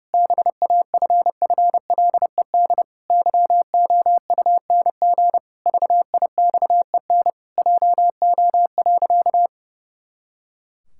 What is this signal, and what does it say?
Morse code, heard as a keyed tone, about 32 wpm, 700 Hz. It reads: BAFFLED YOUNG VIXEN JO.
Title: Pangram Challenge v2 - 30wpm